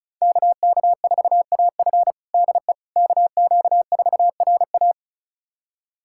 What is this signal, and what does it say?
Morse code, heard as a keyed tone, about 34 wpm, 700 Hz. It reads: KK4AF DE KQ4RA
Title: USA Callsign DE Callsign - 5 Characters - 35wpm